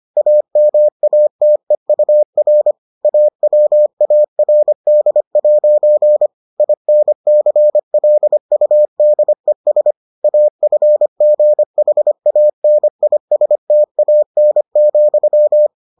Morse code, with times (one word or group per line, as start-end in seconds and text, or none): 0.2-2.7 AMATEUR
3.0-6.3 AWARD'
6.6-9.9 INCLUDES
10.2-15.7 AFGHANISTAN,